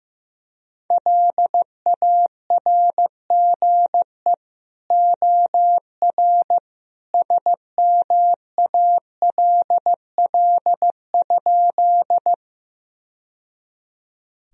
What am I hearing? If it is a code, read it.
Morse code, heard as a keyed tone, about 15 wpm, 700 Hz. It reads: LARGE OR SMALL?
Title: Sentences from Top 1000 Words - Review Part 1 - 15wpm